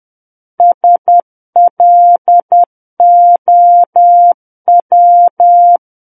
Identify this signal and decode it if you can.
Morse code, heard as a keyed tone, about 10 wpm, 700 Hz. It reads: SLOW
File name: Book One / 01-Chapter 1 - 10 WPM